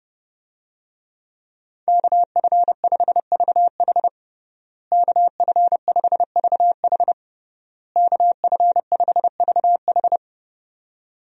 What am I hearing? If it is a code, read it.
Morse code, heard as a keyed tone, about 30 wpm, 700 Hz. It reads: KF5VH KF5VH KF5VH
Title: USA Call Signs - 5 Characters - Repeated 3x - 30wpm